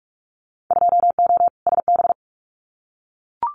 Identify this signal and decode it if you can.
Morse code, heard as a keyed tone, about 44 wpm, 700 Hz. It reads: 2O SB
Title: Winter Field Day Exchanges 45wpm